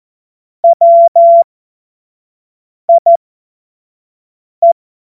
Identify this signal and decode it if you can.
Morse code, heard as a keyed tone, about 14 wpm, spaced as for 5 wpm, 680 Hz. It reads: WIE
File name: Morsekurs / Witze / Witz 03. Mai 2022 25 F70